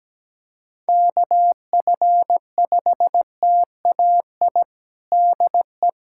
Morse code, heard as a keyed tone, about 17 wpm, 700 Hz. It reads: KF5TAI DE
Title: USA Callsign DE Callsign 17wpm